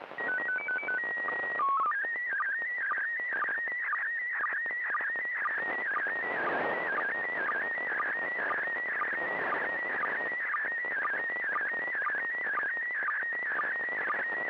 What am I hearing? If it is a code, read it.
SSTV (slow-scan television), PD 120.
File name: SSTV ISS